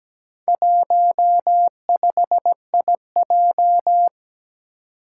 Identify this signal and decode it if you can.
Morse code, heard as a keyed tone, about 17 wpm, 700 Hz. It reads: 15IJ
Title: Four Challenging Letter-Number Combinations 17wpm